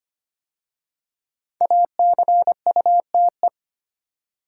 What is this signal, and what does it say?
Morse code, heard as a keyed tone, about 25 wpm, 700 Hz. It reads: ACUTE
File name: 5 Letter Words - First 3 Letters Given - 25wpm